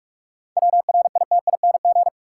Morse code, heard as a keyed tone, about 42 wpm, 700 Hz. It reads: WRITING
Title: Seven Letter Words - 45wpm